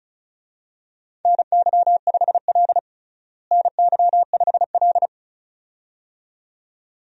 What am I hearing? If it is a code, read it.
Morse code, heard as a keyed tone, about 34 wpm, 700 Hz. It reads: NY5L NY5L
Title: USA Call Signs - 4 Characters - Repeated 2x - 35wpm